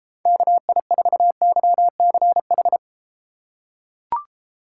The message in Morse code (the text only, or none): KI4YCH